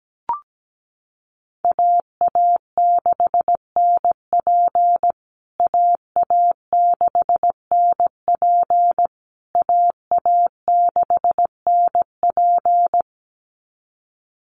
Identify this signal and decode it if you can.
Morse code, heard as a keyed tone, about 17 wpm, 700 Hz. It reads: AA6NP AA6NP AA6NP
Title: USA Call Signs Repeated 3x 17wpm